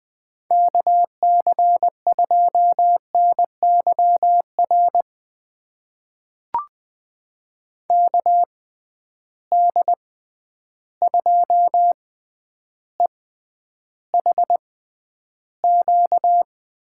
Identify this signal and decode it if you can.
Morse code, heard as a keyed tone, about 20 wpm, spaced as for 10 wpm, 700 Hz. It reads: KC2NYR KD2EHQ
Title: USA Call Signs - 6 Characters - 6x Character Spacing - 20wpm